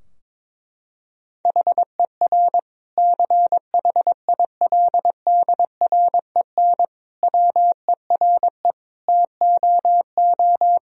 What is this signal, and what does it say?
Morse code, heard as a keyed tone, about 22 wpm, 700 Hz. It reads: HER CHILDREN WERE TOO
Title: Sentences from Top 1000 Words - Review Part 4 - 22wpm